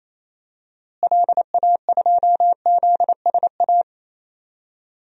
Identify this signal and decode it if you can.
Morse code, heard as a keyed tone, about 28 wpm, 700 Hz. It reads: LA2ZSA
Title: DXCC-100 Call Signs - 6 Characters - Repeated 2x - 28wpm